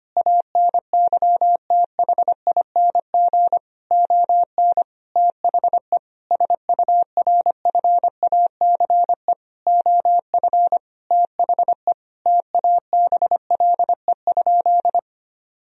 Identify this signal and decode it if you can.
Morse code, heard as a keyed tone, about 25 wpm, 700 Hz. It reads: ANYTHING ON THE SURFACE OF THE TABLE?
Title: Sentences from Top 1000 Words - Review Part 3 - 25wpm